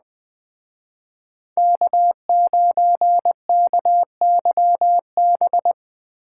Morse code, heard as a keyed tone, about 20 wpm, 700 Hz. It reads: K9KYB